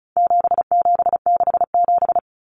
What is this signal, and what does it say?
Morse code, heard as a keyed tone, about 34 wpm, 700 Hz. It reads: 7767
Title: Numbers - 4-Digit 35wpm